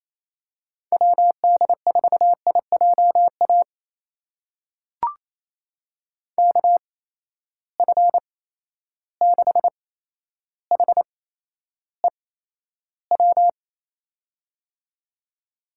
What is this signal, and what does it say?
Morse code, heard as a keyed tone, about 28 wpm, spaced as for 12 wpm, 700 Hz. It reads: WD4IJA KF6HEW
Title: USA Call Signs - 6 Characters - 8x Character Spacing - 28wpm